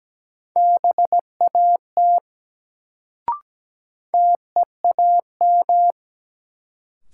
Morse code, heard as a keyed tone, about 17 wpm, 700 Hz. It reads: BAT TEAM
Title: Sentences from Top 1500 Words - New Word Drill Alt - 17wpm